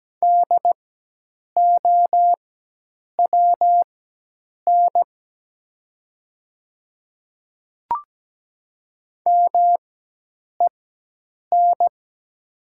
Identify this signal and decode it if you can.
Morse code, heard as a keyed tone, about 17 wpm, spaced as for 8 wpm, 700 Hz. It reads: DOWN MEN
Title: 4x Character Spacing - Top 300 Words - 17wpm